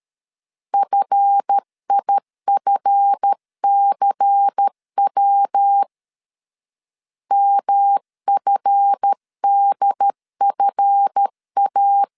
Morse code, heard as a keyed tone, about 13 wpm, 800 Hz. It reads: FIFCW MFDFA
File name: LEZIONI / 7 GF9Z / 7 GF 12wpm 2122